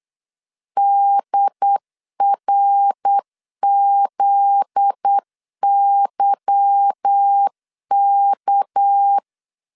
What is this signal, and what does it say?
Morse code, heard as a keyed tone, about 8 wpm, 800 Hz. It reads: DRZYK
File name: LEZIONI / 8 YJ45 / 8 Y 8wpm 3245